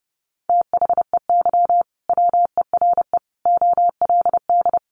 Morse code, heard as a keyed tone, about 30 wpm, 700 Hz. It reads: THEY WERE OLD
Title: Sentences from Top 800 Words 30wpm